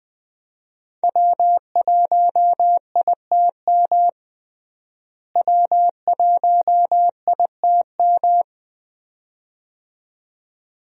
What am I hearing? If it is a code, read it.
Morse code, heard as a keyed tone, about 20 wpm, 700 Hz. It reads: W1ITM W1ITM
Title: USA Call Signs - 5 Characters - Repeated 2x - 20wpm